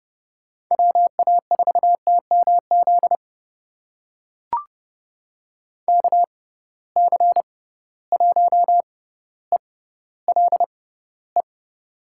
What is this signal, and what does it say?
Morse code, heard as a keyed tone, about 30 wpm, spaced as for 15 wpm, 700 Hz. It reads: WA4TMZ KC1ELE
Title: USA Call Signs - 6 Characters - 6x Character Spacing - 30wpm